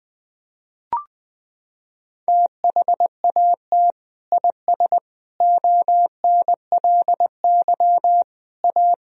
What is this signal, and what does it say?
Morse code, heard as a keyed tone, about 20 wpm, 700 Hz. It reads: THAT IS ONLY A
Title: Sentences from Top 1000 Words - Review Part 4 - 20wpm